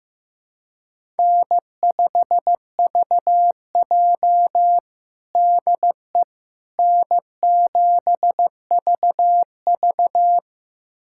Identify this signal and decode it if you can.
Morse code, heard as a keyed tone, about 15 wpm, 700 Hz. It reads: N5VJ DE N7VV
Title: USA Callsign DE Callsign - 4 Characters - 15wpm